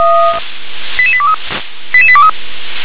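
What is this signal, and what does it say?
Five-tone selective call (ZVEI), heard as quick runs of five tones, tones of about 70 ms. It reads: 88823 88823
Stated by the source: Five-tone ZVEI selective call received on an SDR.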